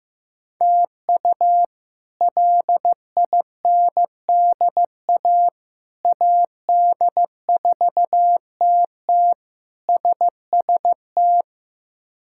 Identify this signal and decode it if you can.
Morse code, heard as a keyed tone, about 15 wpm, 700 Hz. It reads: TU LINDA AD4TT SST
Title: K1USN SST QSO Practice - 15wpm